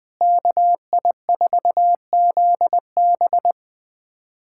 Morse code, heard as a keyed tone, about 21 wpm, 700 Hz. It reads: KI4ZB
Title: USA Callsign DE Callsign - 5 Characters - 20wpm